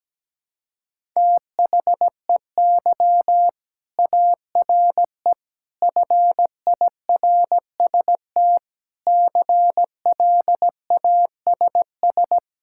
Morse code, heard as a keyed tone, about 17 wpm, 700 Hz. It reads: THEY ARE FIRST CLASS
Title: Sentences from Top 1000 Words - Review Part 5 - 17wpm